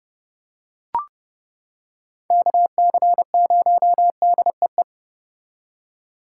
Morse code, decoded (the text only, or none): KC0DEE